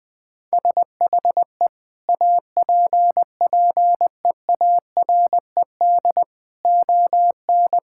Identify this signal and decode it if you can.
Morse code, heard as a keyed tone, about 20 wpm, 700 Hz. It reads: SHE APPEARED ON
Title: Sentences with 5-Word Phrases - Spoken Beginning - 20wpm